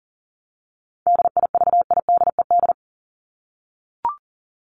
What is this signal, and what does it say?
Morse code, heard as a keyed tone, about 40 wpm, 700 Hz. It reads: DIVIDED
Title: ED Suffix - Top 100 Words with Base Words 40wpm